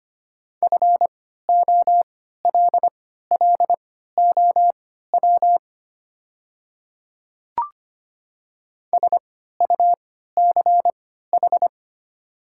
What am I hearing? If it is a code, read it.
Morse code, heard as a keyed tone, about 25 wpm, spaced as for 14 wpm, 700 Hz. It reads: FOLLOW SUCH
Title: 3x Character Spacing - Top 300 Words - 25wpm